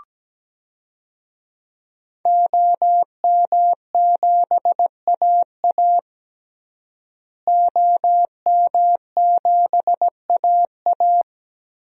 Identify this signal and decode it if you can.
Morse code, heard as a keyed tone, about 17 wpm, 700 Hz. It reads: OM7AA OM7AA
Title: DXCC-100 Call Signs - 5 Characters - Repeated 3x - 17wpm